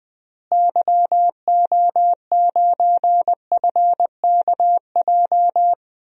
Morse code, heard as a keyed tone, about 20 wpm, 700 Hz. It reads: YO9FKJ
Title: DXCC-100 Call Signs - 6 Characters - Repeated 2x - 20wpm